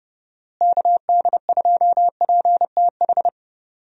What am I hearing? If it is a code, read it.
Morse code, heard as a keyed tone, about 30 wpm, 700 Hz. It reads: KD2PTH
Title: USA Call Signs - 6 Characters - Repeated 3x - 30wpm